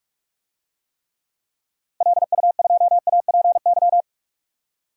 Morse code, heard as a keyed tone, about 41 wpm, 700 Hz. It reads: LU2APY
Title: DXCC-100 Call Signs - 6 Characters - Repeated 2x - 45wpm